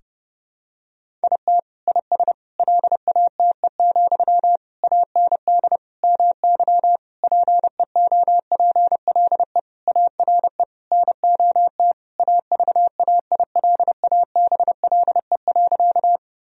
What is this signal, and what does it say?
Morse code, heard as a keyed tone, about 32 wpm, 700 Hz. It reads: IT IS LATE, AND MY PEOPLE ARE NOT AVAILABLE.
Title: Sentences from Top 1000 Words - Review Part 3 - 30wpm